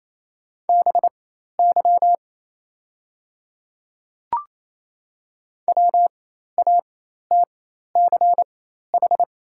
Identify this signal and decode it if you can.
Morse code, heard as a keyed tone, about 28 wpm, spaced as for 13 wpm, 700 Hz. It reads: BY WATCH
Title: 4x Character Spacing - Top 300 Words - 28wpm